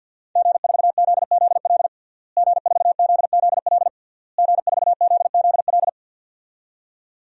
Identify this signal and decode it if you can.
Morse code, heard as a keyed tone, about 51 wpm, 700 Hz. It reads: G4ZZL G4ZZL G4ZZL
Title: DXCC-100 Call Signs - 5 Characters - Repeated 3x - 50wpm